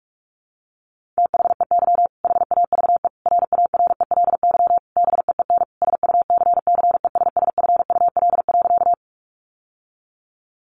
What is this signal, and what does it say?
Morse code, heard as a keyed tone, about 44 wpm, 700 Hz. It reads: THEY HAVE RARELY BEEN SUCCESSFUL.
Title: Sentences from Top 1400 Words 45wpm